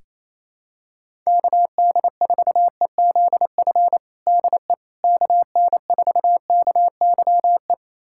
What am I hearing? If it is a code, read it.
Morse code, heard as a keyed tone, about 28 wpm, 700 Hz. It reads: KD4EZF DE KN4KYE
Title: USA Callsign DE Callsign 28wpm